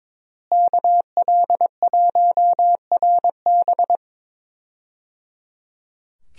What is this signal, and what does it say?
Morse code, heard as a keyed tone, about 22 wpm, 700 Hz. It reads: KL1RB
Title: USA Call Signs - 5 Characters - Repeated 3x - 22wpm